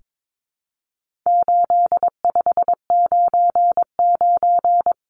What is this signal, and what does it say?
Morse code, heard as a keyed tone, about 22 wpm, 700 Hz. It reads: 8599
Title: Numbers - 4-Digit 22wpm